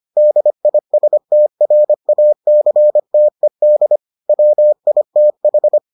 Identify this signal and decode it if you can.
Morse code, heard as a keyed tone, about 25 wpm, 600 Hz. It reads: DISTRACTED WITH